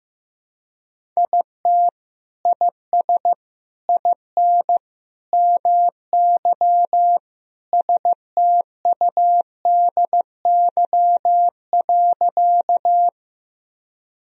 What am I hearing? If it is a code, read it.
Morse code, heard as a keyed tone, about 16 wpm, 700 Hz. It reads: IT IS IN MY STUDY.
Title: Sentences from Top 1000 Words - Review Part 1 - 15wpm